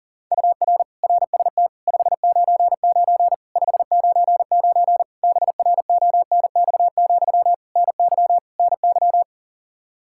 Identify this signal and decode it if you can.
Morse code, heard as a keyed tone, about 40 wpm, 700 Hz. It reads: UR RST 599 599 BRONX, NY NY
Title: Practice QSO - Exchange 1 - Beginning - 40wpm